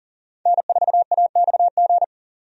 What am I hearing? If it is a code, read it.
Morse code, heard as a keyed tone, about 40 wpm, 700 Hz. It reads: N4AXG
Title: USA Call Signs - 5 Characters - 40wpm